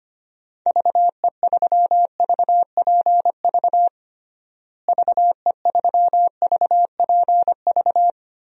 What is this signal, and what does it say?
Morse code, heard as a keyed tone, about 25 wpm, 700 Hz. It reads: VE3VPV VE3VPV
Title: DXCC-100 Call Signs - 6 Characters - Repeated 3x - 25wpm